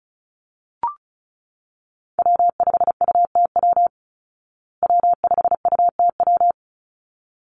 Morse code, heard as a keyed tone, about 34 wpm, 700 Hz. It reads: W5UTW W5UTW
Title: Call Signs Repeated 3x - 35wpm